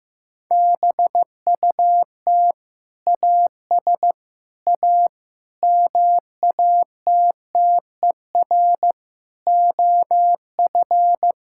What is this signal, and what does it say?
Morse code, heard as a keyed tone, about 15 wpm, 700 Hz. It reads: BUT AS A MATTER OF
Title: Sentences from Top 1000 Words - Review Part 6 - 15wpm